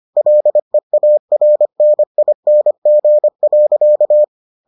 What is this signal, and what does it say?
Morse code, heard as a keyed tone, about 25 wpm, 600 Hz. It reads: LEARNING.